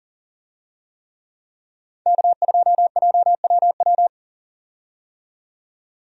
Morse code, heard as a keyed tone, about 40 wpm, 700 Hz. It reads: K2JWW
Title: USA Call Signs - 5 Characters - 40wpm